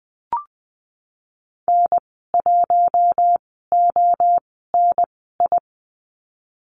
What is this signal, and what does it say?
Morse code, heard as a keyed tone, about 20 wpm, spaced as for 15 wpm, 700 Hz. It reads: N1ONI
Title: 048 Lesson 9g - Number 4 - Call Signs 20wpm